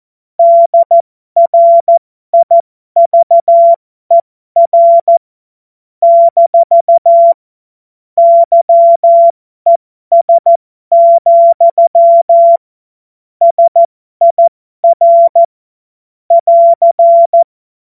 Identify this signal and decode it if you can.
Morse code, heard as a keyed tone, about 14 wpm, spaced as for 12 wpm, 680 Hz. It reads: DRIVER - YES, SIR <AR>